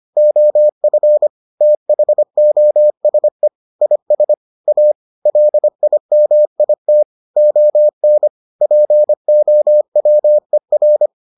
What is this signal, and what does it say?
Morse code, heard as a keyed tone, about 25 wpm, 600 Hz. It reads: OF THOSE IS A LIMIT ON POWER